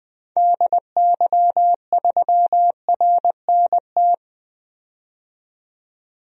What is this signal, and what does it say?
Morse code, heard as a keyed tone, about 20 wpm, 700 Hz. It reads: DY3RNT
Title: DXCC-100 Call Signs - 6 Characters - 20wpm